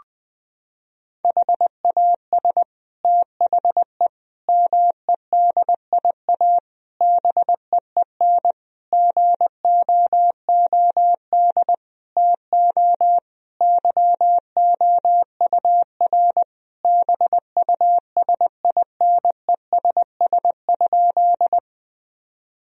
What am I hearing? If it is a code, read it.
Morse code, heard as a keyed tone, about 20 wpm, 700 Hz. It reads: HAS THE MEDIA BEEN GOOD TO YOUR BUSINESS?
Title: Sentences from Top 1000 Words - Review Part 4 - 20wpm